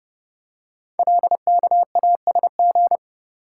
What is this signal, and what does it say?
Morse code, heard as a keyed tone, about 30 wpm, 700 Hz. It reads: LKASG
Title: Code Groups - 5 Letters - 30wpm